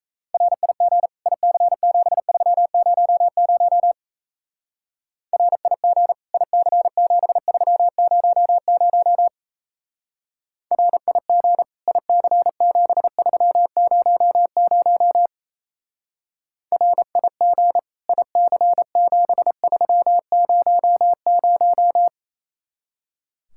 Morse code, heard as a keyed tone, about 33 wpm, 700 Hz. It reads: RIG IC7300 RIG IC7300 RIG IC7300 RIG IC7300
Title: Speed-Racing Common QSO Phrases Encore - 28wpm